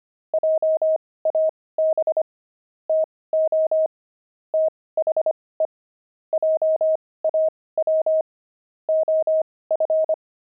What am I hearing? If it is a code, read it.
Morse code, heard as a keyed tone, about 25 wpm, spaced as for 18 wpm, 630 Hz. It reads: JAB TO THE JAW OF
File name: pangram-fs-18-cs-25-gloved